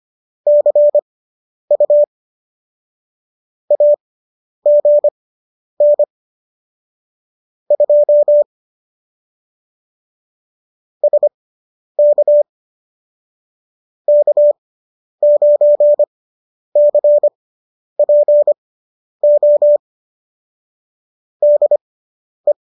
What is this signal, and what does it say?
Morse code, heard as a keyed tone, about 25 wpm, spaced as for 10 wpm, 590 Hz. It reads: CU AGN 2 SK K9CPO DE